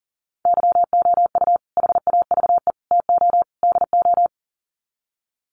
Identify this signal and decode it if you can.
Morse code, heard as a keyed tone, about 40 wpm, 700 Hz. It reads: YOU HAVE TO DO
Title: Sets of 4 Words 40wpm